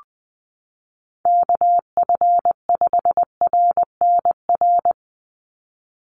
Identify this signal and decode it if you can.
Morse code, heard as a keyed tone, about 20 wpm, 700 Hz. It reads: KF5RNR
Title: USA Call Signs 20wpm